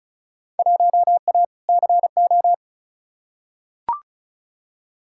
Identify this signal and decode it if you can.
Morse code, heard as a keyed tone, about 36 wpm, 700 Hz. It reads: 1A CO
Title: Field Day Exchanges - 35wpm